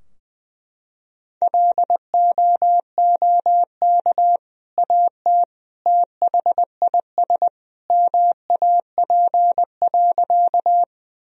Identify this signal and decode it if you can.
Morse code, heard as a keyed tone, about 20 wpm, 700 Hz. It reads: LOOK AT THIS MAP.
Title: Sentences from Top 1000 Words - Review Part 2 - 20wpm